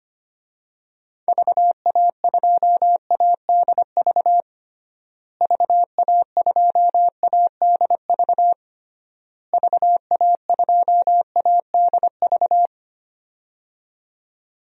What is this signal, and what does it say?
Morse code, heard as a keyed tone, about 25 wpm, 700 Hz. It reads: VA2ADV VA2ADV VA2ADV
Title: DXCC-100 Call Signs - 6 Characters - Repeated 3x - 25wpm